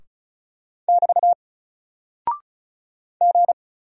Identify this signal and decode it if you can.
Morse code, heard as a keyed tone, about 34 wpm, 700 Hz. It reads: <BT> G
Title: Single Character (Letter:Number:Punctuation:Prosign) Alt 35wpm